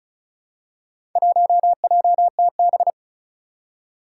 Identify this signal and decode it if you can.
Morse code, heard as a keyed tone, about 34 wpm, 700 Hz. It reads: 1JTB
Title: Four Challenging Letter-Number Combinations 35wpm